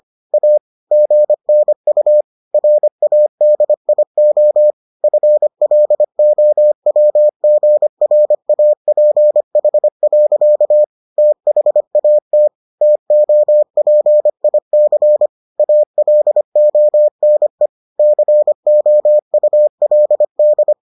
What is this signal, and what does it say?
Morse code, heard as a keyed tone, about 25 wpm, 600 Hz. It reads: A GNU RADIO FLOWGRAPH. THAT TOPIC ALONE COULD